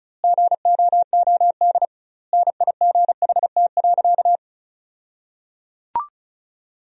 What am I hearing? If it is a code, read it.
Morse code, heard as a keyed tone, about 36 wpm, 700 Hz. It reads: GOOD NIGHT.
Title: 2x Word-Spacing - Sentences from Top 300 Words - 35wpm